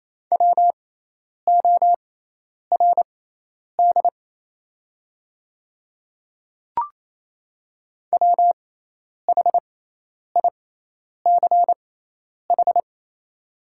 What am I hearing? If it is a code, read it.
Morse code, heard as a keyed tone, about 27 wpm, spaced as for 10 wpm, 690 Hz. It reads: WORD WHICH
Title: 6x Character Spacing - Top 100 Words - 28wpm